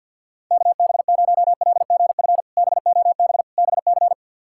Morse code, heard as a keyed tone, about 52 wpm, 700 Hz. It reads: KB0LGF BOB BC
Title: K1USN SST QSO Practice - 50wpm